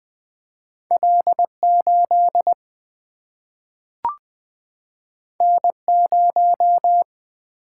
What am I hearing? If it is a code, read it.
Morse code, heard as a keyed tone, about 21 wpm, 700 Hz. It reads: L8 N0
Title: Single-Letter Single-Number 20wpm